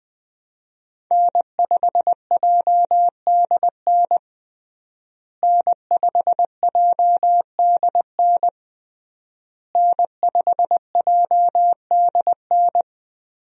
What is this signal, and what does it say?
Morse code, heard as a keyed tone, about 20 wpm, 700 Hz. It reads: N5JDN N5JDN N5JDN
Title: USA Call Signs - 5 Characters - Repeated 3x - 20wpm